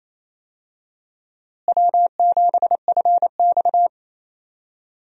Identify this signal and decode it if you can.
Morse code, heard as a keyed tone, about 28 wpm, 700 Hz. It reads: W7FX
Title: USA Call Signs - 4 Characters - Repeated 3x - 28wpm